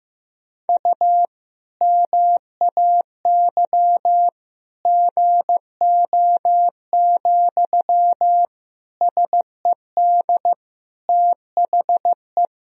Morse code, heard as a keyed tone, about 15 wpm, 700 Hz. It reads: U MAY GO, SED THE
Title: Sentences from Top 400 Words and Abbreviations 15wpm